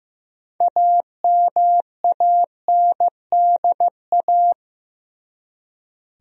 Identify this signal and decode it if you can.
Morse code, heard as a keyed tone, about 15 wpm, 700 Hz. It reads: AMANDA
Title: USA Common Names - 15wpm